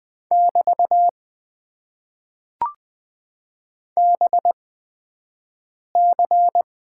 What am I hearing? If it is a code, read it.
Morse code, heard as a keyed tone, about 20 wpm, spaced as for 5 wpm, 700 Hz. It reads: <BT> BC
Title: Top 90 QSO Elements - 8x Character Spacing - 20wpm